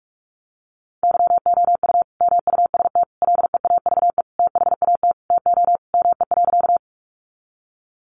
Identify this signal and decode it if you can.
Morse code, heard as a keyed tone, about 44 wpm, 700 Hz. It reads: YOU MUST LEAVE THAT TO ME.
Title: Sentences from Top 400 Words 45wpm